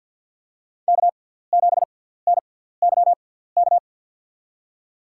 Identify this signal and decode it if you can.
Morse code, heard as a keyed tone, about 49 wpm, spaced as for 17 wpm, 700 Hz. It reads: K7NYK
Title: USA Call Signs - 5 Characters - 6x Character Spacing - 50wpm